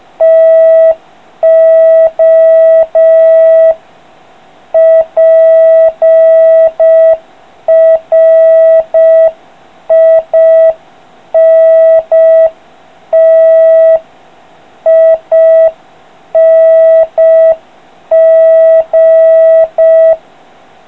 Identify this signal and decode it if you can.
Morse code, heard as a keyed tone, about 5 wpm, 3250 Hz, with noise. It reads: TO PRINTING